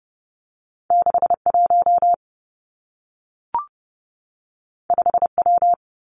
Morse code, heard as a keyed tone, about 30 wpm, 700 Hz. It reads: 6 1 5 W
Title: Two Challenging Letter-Number Combinations 30wpm